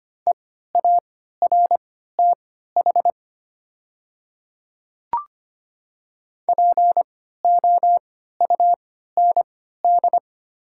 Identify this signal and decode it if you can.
Morse code, heard as a keyed tone, about 25 wpm, spaced as for 14 wpm, 700 Hz. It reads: EARTH POUND
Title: 3x Character Spacing - Top 500 Words - 25wpm